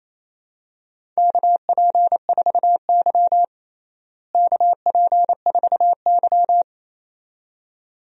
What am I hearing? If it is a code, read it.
Morse code, heard as a keyed tone, about 27 wpm, 700 Hz. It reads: KP4Y KP4Y
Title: DXCC-100 Call Signs - 4 Characters - Repeated 2x - 28wpm